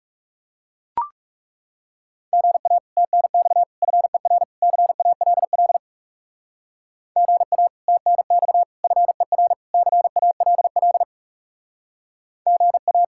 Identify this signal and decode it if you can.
Morse code, heard as a keyed tone, about 40 wpm, 700 Hz. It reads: GA TNX FER CALL GA TNX FER CALL GA